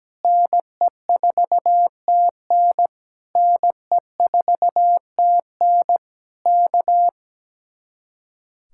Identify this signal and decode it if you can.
Morse code, heard as a keyed tone, about 17 wpm, 700 Hz. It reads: NE4TN NE4TN K
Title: Speed-Racing Common QSO Phrases - 17wpm